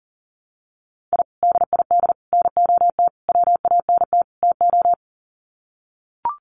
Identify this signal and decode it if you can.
Morse code, heard as a keyed tone, about 40 wpm, 700 Hz. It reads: I DID NOT WANT TO
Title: Sets of 5 Words Encore 40wpm